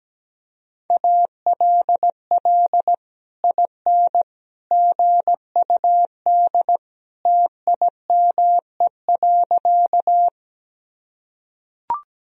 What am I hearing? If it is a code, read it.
Morse code, heard as a keyed tone, about 17 wpm, 700 Hz. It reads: ALL IN GUD TIME.
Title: Sentences from Top 200 Words and Abbreviations 17wpm